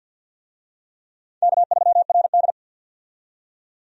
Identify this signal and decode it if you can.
Morse code, heard as a keyed tone, about 47 wpm, 700 Hz. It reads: K3RD
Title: USA Call Signs - 4 Characters - 50wpm